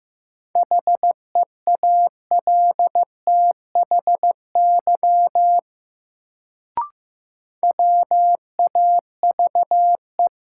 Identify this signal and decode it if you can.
Morse code, heard as a keyed tone, about 15 wpm, 700 Hz. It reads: HEALTHY WAVE
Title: Sentences from Top 1400 Words - New Word Drill Alt - 15wpm